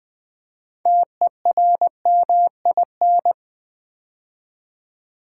Morse code, heard as a keyed tone, about 20 wpm, 700 Hz. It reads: TERMIN
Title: Build A Word - 2 Letters At A Time - 8-Letter Words - 20wpm